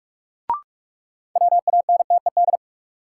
Morse code, heard as a keyed tone, about 47 wpm, 700 Hz. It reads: WANTED
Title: Sets of 2 Words - New Word Drill - 45wpm